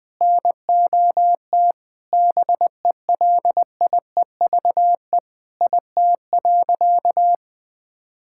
Morse code, heard as a keyed tone, about 20 wpm, 700 Hz. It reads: NOT BELIEVE IT.
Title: Sentences with 4-Word Phrases - Spoken Beginning - 20wpm